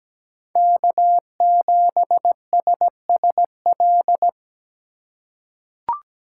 Morse code, heard as a keyed tone, about 17 wpm, 700 Hz. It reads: K7SSL
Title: USA Call Signs - 5 Characters - 2x Character Spacing - 17wpm